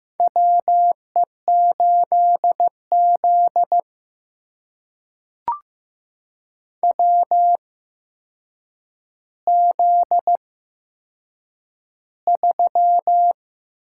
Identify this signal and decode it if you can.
Morse code, heard as a keyed tone, about 15 wpm, spaced as for 7 wpm, 700 Hz. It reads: WE8Z WZ3
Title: USA Call Signs - 4 Characters - 8x Character Spacing - 15wpm